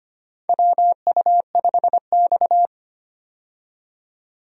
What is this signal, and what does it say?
Morse code, heard as a keyed tone, about 25 wpm, 700 Hz. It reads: WU5X
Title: USA Call Signs - 4 Characters - 25wpm